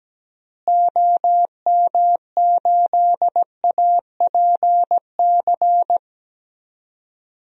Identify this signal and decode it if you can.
Morse code, heard as a keyed tone, about 17 wpm, 700 Hz. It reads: OM8APC
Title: DXCC-100 Call Signs - 6 Characters - Repeated 3x - 17wpm